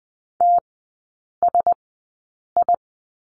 Morse code, heard as a keyed tone, about 20 wpm, spaced as for 8 wpm, 700 Hz. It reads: TSI
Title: 074 Lesson 13e - Letter L - Three Characters 20wpm